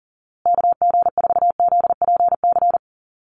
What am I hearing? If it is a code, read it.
Morse code, heard as a keyed tone, about 40 wpm, 700 Hz. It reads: KG4ZPC
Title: USA Call Signs Repeated 2x 40wpm